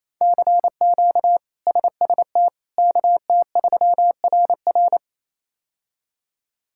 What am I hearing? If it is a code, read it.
Morse code, heard as a keyed tone, about 28 wpm, 700 Hz. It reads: CQ SST KT3RR
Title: K1USN SST QSO Practice - 28wpm